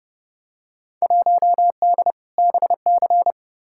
Morse code, heard as a keyed tone, about 30 wpm, 700 Hz. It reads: 1D BC
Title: Field Day Exchanges - 30wpm